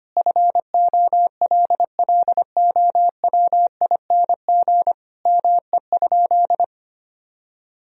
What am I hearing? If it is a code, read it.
Morse code, heard as a keyed tone, about 25 wpm, 700 Hz. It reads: FOLLOWING ME?
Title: Sentences from Top 1000 Words - Review Part 4 - 25wpm